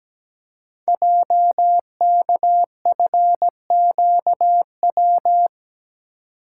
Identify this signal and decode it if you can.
Morse code, heard as a keyed tone, about 17 wpm, 700 Hz. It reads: JKFQW
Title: Code Groups - 5 Letters - 17wpm